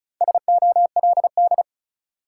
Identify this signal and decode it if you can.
Morse code, heard as a keyed tone, about 36 wpm, 700 Hz. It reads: SOLD
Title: Sentences from Top 1400 Words - New Word Drill - 35wpm